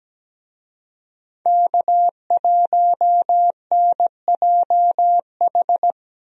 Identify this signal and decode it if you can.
Morse code, heard as a keyed tone, about 17 wpm, 700 Hz. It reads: K1NJH
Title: USA Call Signs - 5 Characters - Repeated 3x - 17wpm